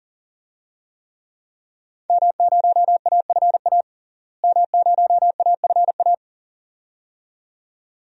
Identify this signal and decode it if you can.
Morse code, heard as a keyed tone, about 40 wpm, 700 Hz. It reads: M0AFA M0AFA
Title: DXCC-100 Call Signs - 5 Characters - Repeated 2x - 40wpm